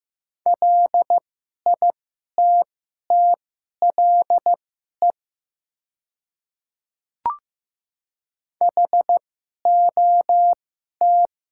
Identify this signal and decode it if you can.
Morse code, heard as a keyed tone, about 15 wpm, spaced as for 11 wpm, 700 Hz. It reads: LITTLE HOT
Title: 2x Character Spacing - Top 300 Words - 15wpm